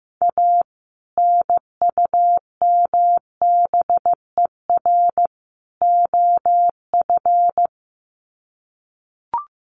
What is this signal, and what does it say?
Morse code, heard as a keyed tone, about 15 wpm, 700 Hz. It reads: A NUMBER OF
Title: Sets of 3 Words 15wpm